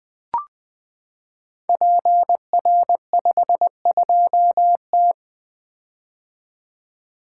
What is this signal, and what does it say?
Morse code, heard as a keyed tone, about 20 wpm, 700 Hz. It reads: PR52T
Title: Code Groups - 5 Letter:Numbers - 20wpm